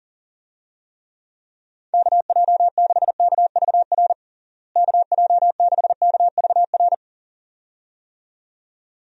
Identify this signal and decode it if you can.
Morse code, heard as a keyed tone, about 40 wpm, 700 Hz. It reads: KJ6KVR KJ6KVR
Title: USA Call Signs - 6 Characters - Repeated 2x - 40wpm